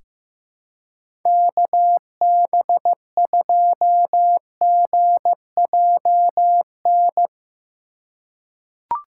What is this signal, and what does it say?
Morse code, heard as a keyed tone, about 15 wpm, 700 Hz. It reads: KB2GJN
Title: USA Call Signs - 6 Characters - 2x Character Spacing - 15wpm